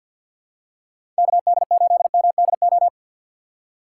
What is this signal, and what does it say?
Morse code, heard as a keyed tone, about 51 wpm, 700 Hz. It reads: KD8MDO
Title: USA Call Signs - 4x Character Spacing - 50wpm